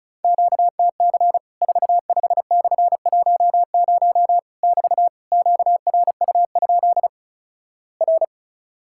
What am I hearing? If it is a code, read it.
Morse code, heard as a keyed tone, about 36 wpm, 700 Hz. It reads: QTC 45/10 <BT> QRU? R
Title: Worked All Europe DX Contest - QTC Exchange - 35wpm